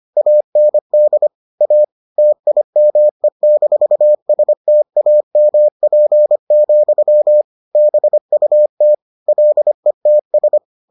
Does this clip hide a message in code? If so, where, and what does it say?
Yes: Morse code from 0.2 to 10.6 s, heard as a keyed tone, about 25 wpm, 600 Hz. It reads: AND A TIME-STAMP, BUT LETS